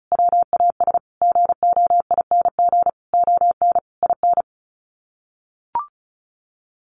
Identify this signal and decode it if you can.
Morse code, heard as a keyed tone, about 34 wpm, 700 Hz. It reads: WAS GOING ON IN
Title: Sets of 5 Words Encore 35wpm